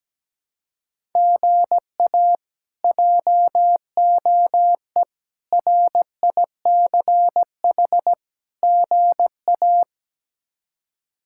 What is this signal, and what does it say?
Morse code, heard as a keyed tone, about 17 wpm, 700 Hz. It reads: GA JOE RICH GA
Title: K1USN SST QSO Practice - 17wpm